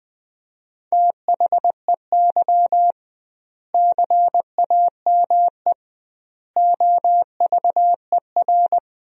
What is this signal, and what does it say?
Morse code, heard as a keyed tone, about 21 wpm, 700 Hz. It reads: THEY CAME OVER
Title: 2x Word-Spacing - Sentences from Top 300 Words - 20wpm